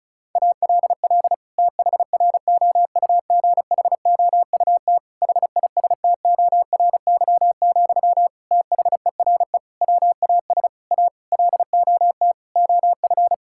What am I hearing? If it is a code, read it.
Morse code, heard as a keyed tone, about 36 wpm, 700 Hz. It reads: ALL THROUGHOUT HISTORY, THERE WAS A LOT OF
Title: Sentences from Top 1000 Words - Review Part 5 - 35wpm